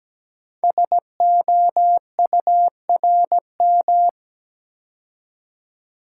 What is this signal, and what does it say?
Morse code, heard as a keyed tone, about 17 wpm, 700 Hz. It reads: SOURM
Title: Code Groups - 5 Letters - 17wpm